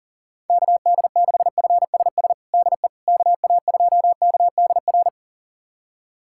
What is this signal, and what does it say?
Morse code, heard as a keyed tone, about 40 wpm, 700 Hz. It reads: KD6FSS DE KA2KDR
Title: USA Callsign DE Callsign 40wpm